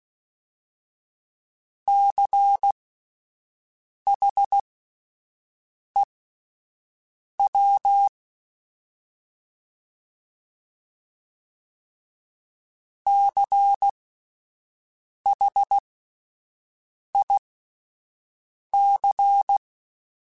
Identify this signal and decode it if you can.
Morse code, heard as a keyed tone, about 16 wpm, spaced as for 5 wpm, 800 Hz. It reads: CHEW CHIC
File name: Spaced Alphas / Spaced Alphas mp3s / Cs sp @10